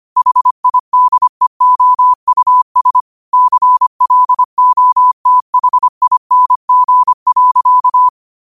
Morse code, heard as a keyed tone, about 25 wpm, 1000 Hz. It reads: HIDEOUS CLOTHING.